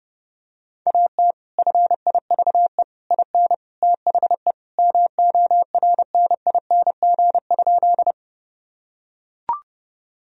Morse code, heard as a keyed tone, about 30 wpm, 700 Hz. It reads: AT FIVE IN THE MORNING?
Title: Sentences from Top 1000 Words - Review Part 2 - 30wpm